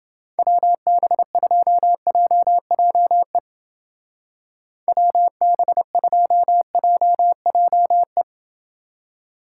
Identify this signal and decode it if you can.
Morse code, heard as a keyed tone, about 28 wpm, 700 Hz. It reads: WB2JJE WB2JJE